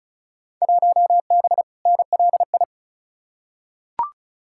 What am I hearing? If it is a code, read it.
Morse code, heard as a keyed tone, about 34 wpm, 700 Hz. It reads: 1B NLI
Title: Field Day Exchanges - 35wpm